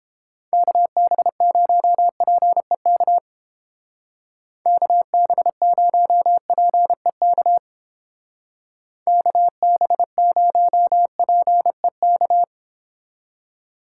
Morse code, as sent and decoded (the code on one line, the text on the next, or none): -.- -... ----- .--. -.- / -.- -... ----- .--. -.- / -.- -... ----- .--. -.-
KB0PEK KB0PEK KB0PEK